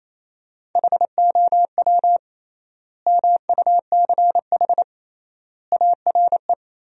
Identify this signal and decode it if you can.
Morse code, heard as a keyed tone, about 27 wpm, 700 Hz. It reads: HOW MUCH ARE
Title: 3x Word-Spacing - Sentences from Top 200 Words - 28wpm